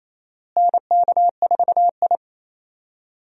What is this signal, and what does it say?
Morse code, heard as a keyed tone, about 28 wpm, 700 Hz. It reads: NK4I
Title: USA Call Signs - 4 Characters - Repeated 2x - 28wpm